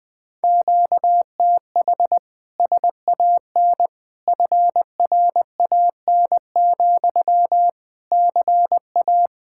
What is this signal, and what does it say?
Morse code, heard as a keyed tone, about 20 wpm, 700 Hz. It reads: QTH SAN FRAN, CA
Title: Speed-Racing Common QSO Phrases - 20wpm